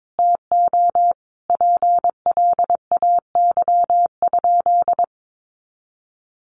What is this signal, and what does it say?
Morse code, heard as a keyed tone, about 22 wpm, 700 Hz. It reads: TO PLAY?
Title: Sentences from Top 400 Words 22wpm